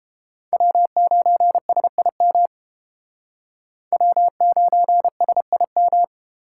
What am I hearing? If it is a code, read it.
Morse code, heard as a keyed tone, about 30 wpm, 700 Hz. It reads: W9SIM W9SIM